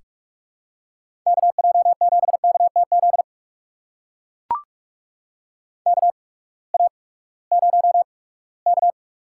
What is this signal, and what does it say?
Morse code, heard as a keyed tone, about 44 wpm, spaced as for 22 wpm, 700 Hz. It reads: KJ7KTZ KA0K
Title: USA Call Signs - 8x Character Spacing - 45wpm